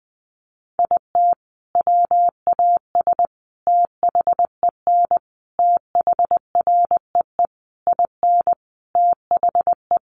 Morse code, heard as a keyed tone, about 20 wpm, 700 Hz. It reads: IT WAS THEN THREE IN THE
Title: Sentences from Top 1500 Words 20wpm